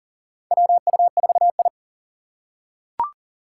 Morse code, heard as a keyed tone, about 43 wpm, 700 Hz. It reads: WU4I
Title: USA Call Signs - 4 Characters - 4x Character Spacing - 40wpm